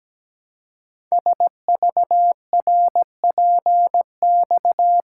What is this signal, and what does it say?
Morse code, heard as a keyed tone, about 17 wpm, 700 Hz. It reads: SVRPX
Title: Code Groups - 5 Letters - 17wpm